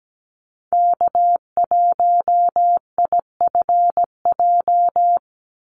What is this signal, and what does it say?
Morse code, heard as a keyed tone, about 17 wpm, 700 Hz. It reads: K1IFJ
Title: Call Signs - 17wpm